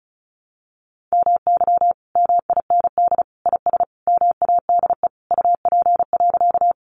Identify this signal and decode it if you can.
Morse code, heard as a keyed tone, about 34 wpm, 700 Hz. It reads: MY MIND IS MADE UP.
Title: Sentences from Top 500 Words 35wpm